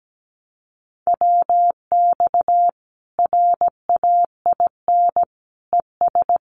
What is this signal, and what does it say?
Morse code, heard as a keyed tone, about 17 wpm, 700 Hz. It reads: WX RAIN ES
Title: Common QSO Phrases 17wpm